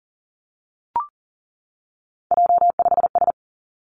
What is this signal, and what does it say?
Morse code, heard as a keyed tone, about 37 wpm, 700 Hz. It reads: J5S
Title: Three Challenging Letter-Number Combinations 40wpm